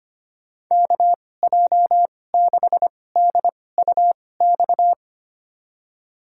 Morse code, heard as a keyed tone, about 25 wpm, spaced as for 18 wpm, 700 Hz. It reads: KJ6DUX
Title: USA Call Signs - 6 Characters - 2x Character Spacing - 25wpm